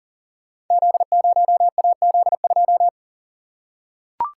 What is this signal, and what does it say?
Morse code, heard as a keyed tone, about 39 wpm, 700 Hz. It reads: Z0AZ2
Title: Code Groups - 5 Letter:Numbers - 40wpm